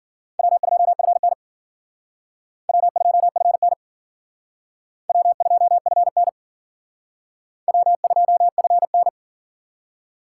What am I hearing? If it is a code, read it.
Morse code, heard as a keyed tone, about 49 wpm, 700 Hz. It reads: W2FN W2FN W2FN W2FN
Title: Speed Racing - USA Call Signs 40wpm